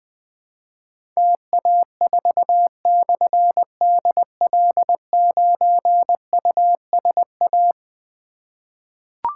Morse code, heard as a keyed tone, about 21 wpm, 700 Hz. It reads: TA4/DL9USA
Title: Difficult Call Signs 20wpm